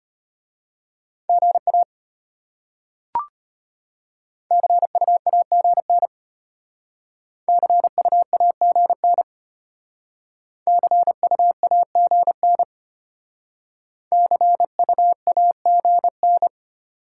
Morse code, heard as a keyed tone, about 32 wpm, 700 Hz. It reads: GA CUAGN CUAGN CUAGN CUAGN